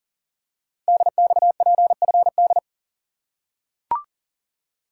Morse code, heard as a keyed tone, about 40 wpm, 690 Hz. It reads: DXPFD
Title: Code Groups - 5 Letters - 40wpm